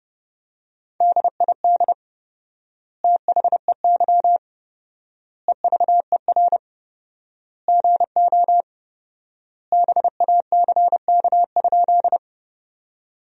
Morse code, heard as a keyed tone, about 30 wpm, 700 Hz. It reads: DID THEY EVER GO BACK?
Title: 4x Word-Spacing - Sentences from Top 400 Words - 30wpm